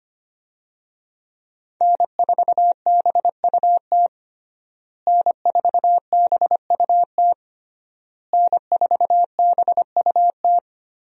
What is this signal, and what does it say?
Morse code, heard as a keyed tone, about 25 wpm, 700 Hz. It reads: N4BUT N4BUT N4BUT
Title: USA Call Signs - 5 Characters - Repeated 3x - 25wpm